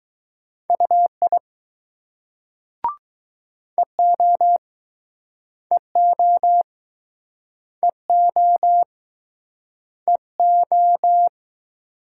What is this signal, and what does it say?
Morse code, heard as a keyed tone, about 20 wpm, 700 Hz. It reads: UI EO EO EO EO